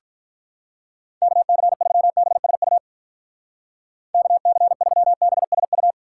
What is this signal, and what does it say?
Morse code, heard as a keyed tone, about 45 wpm, 700 Hz. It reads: KC3BSU KC3BSU